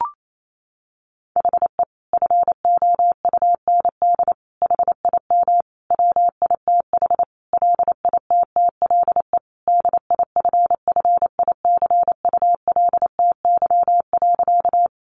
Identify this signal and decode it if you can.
Morse code, heard as a keyed tone, about 28 wpm, 700 Hz. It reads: HE FOUND HIM WITH LITTLE DIFFICULTY.
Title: Sentences from Top 1500 Words 28wpm